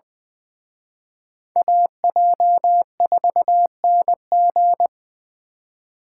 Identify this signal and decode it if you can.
Morse code, heard as a keyed tone, about 24 wpm, 700 Hz. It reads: AJ4NG
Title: Speed Racing - USA 2x2 Call Signs 20wpm